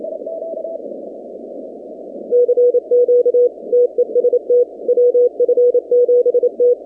Morse code, heard as a keyed tone, about 27 wpm, 490 Hz. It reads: CQ TEST WF7T
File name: 2019-cw-wpx-kt5j-k5pi